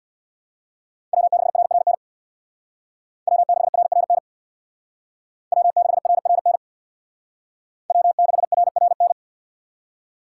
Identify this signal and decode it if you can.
Morse code, heard as a keyed tone, about 59 wpm, 700 Hz. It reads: W6RRN W6RRN W6RRN W6RRN
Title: Speed Racing - USA 1x3 Call Signs 50wpm